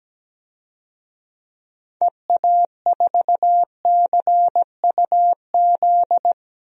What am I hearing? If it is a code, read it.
Morse code, heard as a keyed tone, about 17 wpm, 700 Hz. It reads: EA4CUZ
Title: DXCC-100 Call Signs - 6 Characters - 17wpm